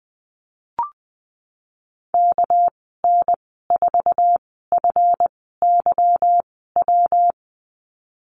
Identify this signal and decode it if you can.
Morse code, heard as a keyed tone, about 19 wpm, spaced as for 14 wpm, 700 Hz. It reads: KN4FYW
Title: 213 Lesson 34g - Number 8 - Call Signs 20wpm